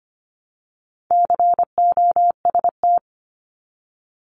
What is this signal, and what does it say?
Morse code, heard as a keyed tone, about 25 wpm, 700 Hz. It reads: COST
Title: Multi-letter Phonemes 25wpm